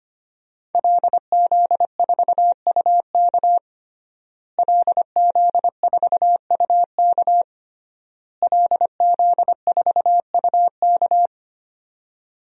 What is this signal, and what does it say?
Morse code, heard as a keyed tone, about 25 wpm, 700 Hz. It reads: LZ4UK LZ4UK LZ4UK
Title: DXCC-100 Call Signs - 5 Characters - Repeated 3x - 25wpm